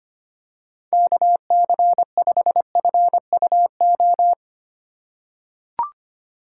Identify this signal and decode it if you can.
Morse code, heard as a keyed tone, about 25 wpm, 700 Hz. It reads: KC5FUO
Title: USA Call Signs - 4x Character Spacing - 25wpm